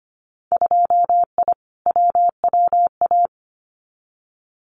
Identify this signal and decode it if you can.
Morse code, heard as a keyed tone, about 25 wpm, 700 Hz. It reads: 2I WWA
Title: Winter Field Day Exchanges Encore 25wpm